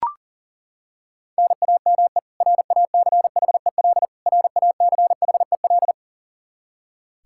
Morse code, heard as a keyed tone, about 40 wpm, 700 Hz. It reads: NAME RACHEL RACHEL
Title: Practice QSO - Exchange 1 - Beginning - 40wpm